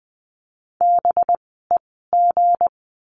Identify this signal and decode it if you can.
Morse code, heard as a keyed tone, about 20 wpm, spaced as for 14 wpm, 700 Hz. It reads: BEG
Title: 185 Lesson 30f - Letter b - Words 20wpm